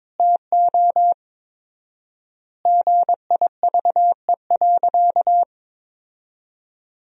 Morse code, heard as a keyed tone, about 22 wpm, 700 Hz. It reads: TO GIVE.
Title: 4x Word-Spacing - Sentences from Top 200 Words - 22wpm